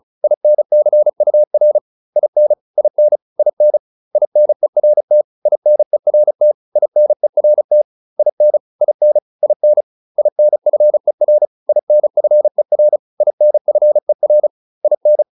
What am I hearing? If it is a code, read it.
Morse code, heard as a keyed tone, about 35 wpm, 600 Hz. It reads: INCUR IN IN IN INERT INERT INERT IN IN IN INFER INFER INFER IN